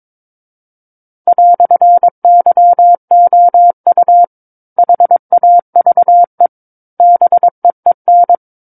Morse code, heard as a keyed tone, about 22 wpm, 700 Hz. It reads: "YOU HAVE BEEN
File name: Book One / 13-Chapter 13 - 22 WPM